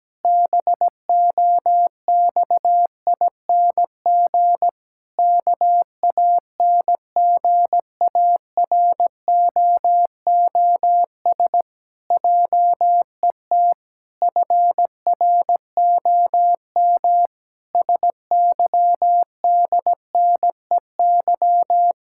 Morse code, heard as a keyed tone, about 17 wpm, 700 Hz. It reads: BOXING KANGAROOS JET FROM SYDNEY
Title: Pangram Challenge v2 - 17wpm